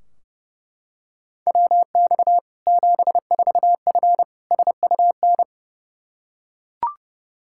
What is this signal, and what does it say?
Morse code, heard as a keyed tone, about 30 wpm, 700 Hz. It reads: WX 74F SUN
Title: Practice QSO - Exchange 2 - Info - 30wpm